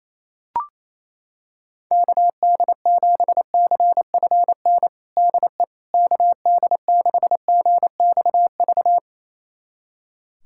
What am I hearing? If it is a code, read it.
Morse code, heard as a keyed tone, about 27 wpm, 700 Hz. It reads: KD7CFN DE KD6GXV
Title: USA Callsign DE Callsign 28wpm